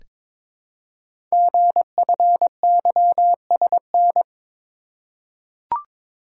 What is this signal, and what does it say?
Morse code, heard as a keyed tone, about 22 wpm, 700 Hz. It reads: GFYSN
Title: Code Groups - 5 Letter:Numbers - 22wpm